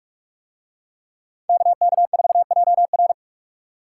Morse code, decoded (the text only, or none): KK4JR